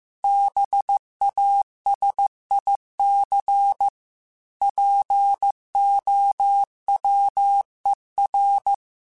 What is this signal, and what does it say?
Morse code, heard as a keyed tone, about 15 wpm, 800 Hz. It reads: BASIC POWER
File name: Technical Subjects / Technical Subjects-mp3s / Sw Pwr Sup @13